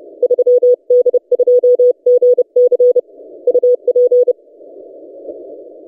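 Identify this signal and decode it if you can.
Morse code, heard as a keyed tone, about 30 wpm, 490 Hz. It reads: 3D2GC UP
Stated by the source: DX station 3D2GC on 20 metres CW, recorded by N6ML.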